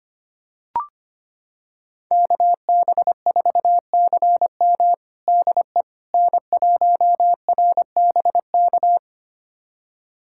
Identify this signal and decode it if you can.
Morse code, heard as a keyed tone, about 25 wpm, 700 Hz. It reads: KB4CM DE N1RBK
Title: USA Callsign DE Callsign - 5 Characters - 25wpm